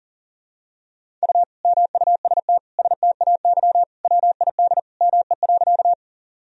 Morse code, heard as a keyed tone, about 40 wpm, 700 Hz. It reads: U MUST STAY WID ME.
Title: Sentences from Top 500 Words and Abbreviations 40wpm